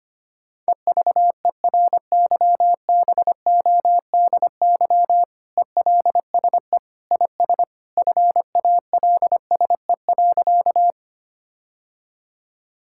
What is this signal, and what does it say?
Morse code, heard as a keyed tone, about 25 wpm, 700 Hz. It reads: EVERYBODY ELSE IS FALSE.
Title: Sentences from Top 1600 Words 25wpm